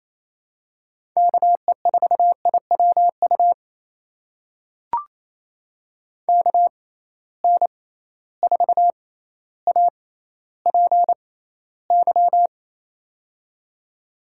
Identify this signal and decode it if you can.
Morse code, heard as a keyed tone, about 28 wpm, spaced as for 14 wpm, 700 Hz. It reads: KE4IWU KN4APY
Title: USA Call Signs - 6 Characters - 6x Character Spacing - 28wpm